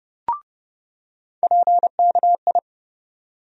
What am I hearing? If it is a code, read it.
Morse code, heard as a keyed tone, about 30 wpm, 700 Hz. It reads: PKI
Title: USA Call Sign Postfixes - 3 Letters 30wpm